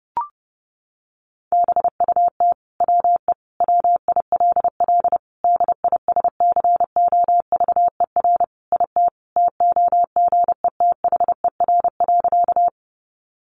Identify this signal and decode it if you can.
Morse code, heard as a keyed tone, about 30 wpm, 700 Hz. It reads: BUT WE WILL DISCOVER IT TOGETHER.
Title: Sentences from Top 1100 Words 30wpm